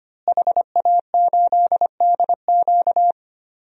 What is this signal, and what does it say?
Morse code, heard as a keyed tone, about 25 wpm, 700 Hz. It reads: HA8DQ
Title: DXCC-100 Call Signs - 5 Characters - Repeated 3x - 25wpm